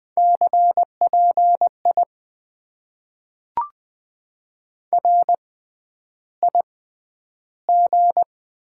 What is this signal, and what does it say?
Morse code, heard as a keyed tone, about 20 wpm, spaced as for 10 wpm, 700 Hz. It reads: CPI RIG
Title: Top 90 QSO Elements - 6x Character Spacing - 20wpm